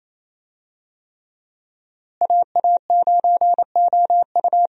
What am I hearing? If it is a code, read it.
Morse code, heard as a keyed tone, about 28 wpm, 700 Hz. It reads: AA9OU
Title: USA Call Signs - 5 Characters - Repeated 3x - 28wpm